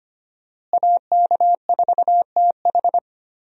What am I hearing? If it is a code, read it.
Morse code, heard as a keyed tone, about 25 wpm, 700 Hz. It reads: AK4TH
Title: USA Call Signs - 5 Characters - Repeated 3x - 25wpm